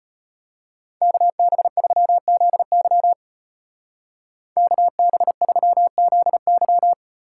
Morse code, heard as a keyed tone, about 32 wpm, 700 Hz. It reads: KB3ZY KB3ZY